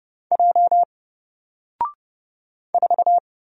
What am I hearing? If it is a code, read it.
Morse code, heard as a keyed tone, about 29 wpm, 700 Hz. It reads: J 4
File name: Single-Letter-Number-30wpm